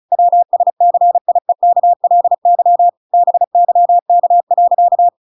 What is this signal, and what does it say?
Morse code, heard as a keyed tone, about 34 wpm, 700 Hz. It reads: WSCIEKLY BYK.